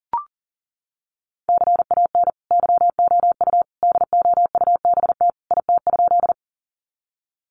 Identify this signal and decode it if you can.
Morse code, heard as a keyed tone, about 40 wpm, 700 Hz. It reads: CAN YOU DOUBT IT?
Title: Sentences from Top 1500 Words 40wpm